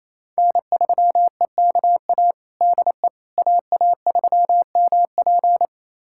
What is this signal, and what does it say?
Morse code, heard as a keyed tone, about 28 wpm, 700 Hz. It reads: N3EKA DE AA3MP
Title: USA Callsign DE Callsign - 5 Characters - 28wpm